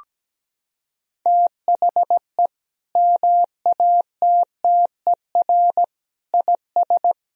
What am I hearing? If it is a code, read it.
Morse code, heard as a keyed tone, about 17 wpm, 700 Hz. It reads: THE MATTER IS
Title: Sentences from Top 1000 Words - Review Part 4 - 17wpm